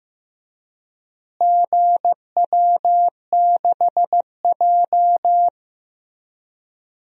Morse code, heard as a keyed tone, about 15 wpm, 700 Hz. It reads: GW6J
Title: DXCC-100 Call Signs - 4 Characters - Repeated 3x - 15wpm